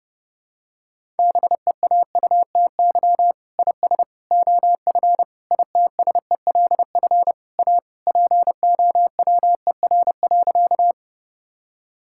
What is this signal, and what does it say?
Morse code, heard as a keyed tone, about 32 wpm, 700 Hz. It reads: BEAUTY IS OF ITSELF A POWER.
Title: Sentences from Top 1000 Words - Review Part 5 - 30wpm